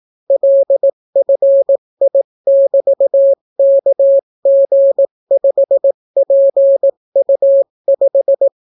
Morse code, heard as a keyed tone, about 18 wpm, 550 Hz. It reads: LFI<BT>KG5PU5